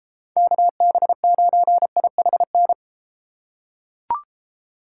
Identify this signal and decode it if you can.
Morse code, heard as a keyed tone, about 32 wpm, 700 Hz. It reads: KB9IHN